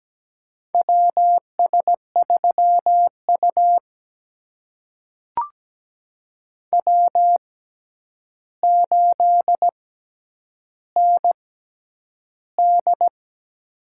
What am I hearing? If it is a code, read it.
Morse code, heard as a keyed tone, about 17 wpm, spaced as for 9 wpm, 700 Hz. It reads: WS3U W8ND
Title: USA Call Signs - 4 Characters - 6x Character Spacing - 17wpm